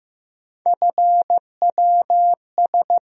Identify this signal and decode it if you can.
Morse code, heard as a keyed tone, about 15 wpm, 700 Hz. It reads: FWS
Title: USA Call Sign Postfixes - 3 Letters 15wpm